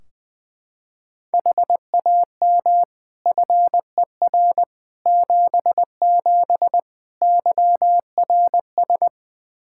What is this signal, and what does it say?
Morse code, heard as a keyed tone, about 20 wpm, 700 Hz. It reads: HAM FER 77 YRS
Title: K1USN SST QSO Practice 20wpm